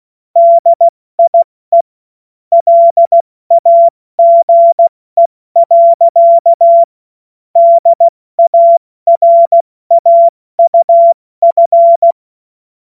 Morse code, heard as a keyed tone, about 16 wpm, spaced as for 14 wpm, 680 Hz. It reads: DIE LAGE. DARAUF
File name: Morsekurs / Witze / Witz 14. Dezember 2021, 70 F80